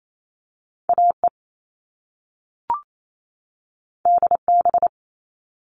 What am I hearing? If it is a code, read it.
Morse code, heard as a keyed tone, about 28 wpm, 700 Hz. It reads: AE DB
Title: All Cut Numbers - 2-Digit 28wpm